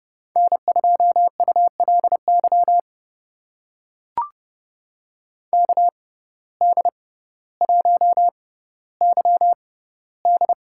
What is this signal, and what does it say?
Morse code, heard as a keyed tone, about 30 wpm, spaced as for 15 wpm, 700 Hz. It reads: N2ULY KD1YD
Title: USA Call Signs - 5 Characters - 6x Character Spacing - 30wpm